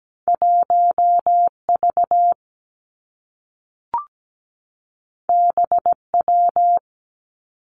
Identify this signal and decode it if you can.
Morse code, heard as a keyed tone, about 17 wpm, 700 Hz. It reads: 1 V B W
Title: Two Challenging Letter-Number Combinations 17wpm